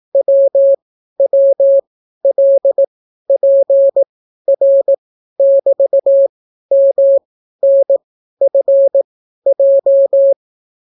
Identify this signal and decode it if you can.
Morse code, heard as a keyed tone, about 18 wpm, spaced as for 12 wpm, 550 Hz. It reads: WWLPR<BT>MNFJ